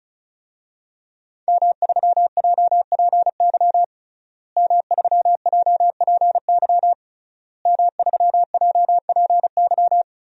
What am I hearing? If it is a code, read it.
Morse code, heard as a keyed tone, about 36 wpm, 700 Hz. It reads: M3JPY M3JPY M3JPY
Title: DXCC-100 Call Signs - 5 Characters - Repeated 3x - 35wpm